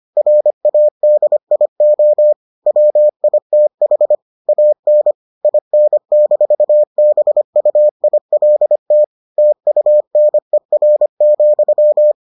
Morse code, heard as a keyed tone, about 25 wpm, 600 Hz. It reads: RADIO WITH AN IN-BUILT TUNER,